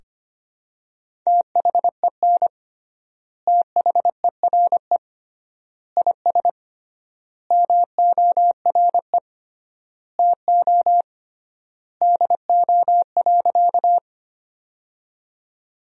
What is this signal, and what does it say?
Morse code, heard as a keyed tone, about 25 wpm, 700 Hz. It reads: THEN THERE IS MORE TO DO.
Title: 3x Word-Spacing - Sentences from Top 100 Words - 25wpm